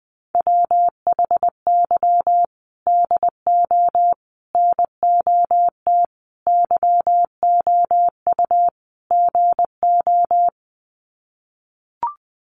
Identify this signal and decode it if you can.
Morse code, heard as a keyed tone, about 20 wpm, 700 Hz. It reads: WHY DO NOT YOU GO
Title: Sets of 5 Words Encore 20wpm